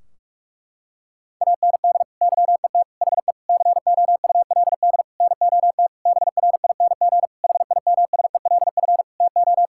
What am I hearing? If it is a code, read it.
Morse code, heard as a keyed tone, about 41 wpm, 700 Hz. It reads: AND YET HE COULD NOT BRING HIMSELF TO
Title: Sentences from Top 1600 Words 45wpm